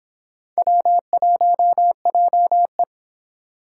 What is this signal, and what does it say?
Morse code, heard as a keyed tone, about 29 wpm, 700 Hz. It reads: W1JE